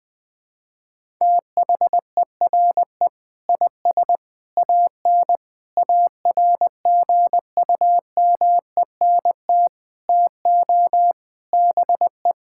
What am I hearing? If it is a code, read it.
Morse code, heard as a keyed tone, about 20 wpm, 700 Hz. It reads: THERE IS AN ARGUMENT TO BE
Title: Sets of 8 Words - 20wpm